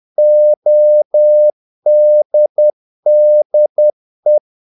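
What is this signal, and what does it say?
Morse code, heard as a keyed tone, about 10 wpm, 600 Hz. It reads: ODDE
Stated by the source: Morse practice of short Danish words at 10 wpm, from CW med Gnister.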